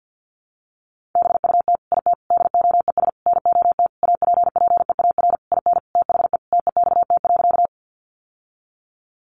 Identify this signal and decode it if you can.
Morse code, heard as a keyed tone, about 50 wpm, 700 Hz. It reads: BUT IT DOES NOT APPEAR IN THE TEXT.
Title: Sentences from Top 1200 Words 50wpm